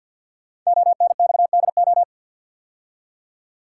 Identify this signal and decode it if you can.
Morse code, heard as a keyed tone, about 48 wpm, 690 Hz. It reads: ONXDO
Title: Code Groups - 5 Letters - 50wpm